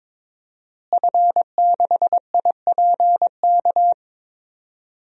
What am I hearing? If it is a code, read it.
Morse code, heard as a keyed tone, about 22 wpm, 700 Hz. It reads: F6IPK
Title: DXCC-100 Call Signs - 5 Characters - Repeated 2x - 22wpm